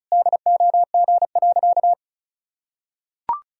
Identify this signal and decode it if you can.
Morse code, heard as a keyed tone, about 36 wpm, 700 Hz. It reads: DOG.
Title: Sentences from Top 1000 Words - Review Part 2 - 35wpm